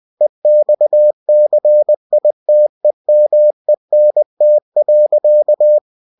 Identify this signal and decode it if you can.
Morse code, heard as a keyed tone, about 20 wpm, 600 Hz. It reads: EXCITEMENT.
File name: winnie-20wpm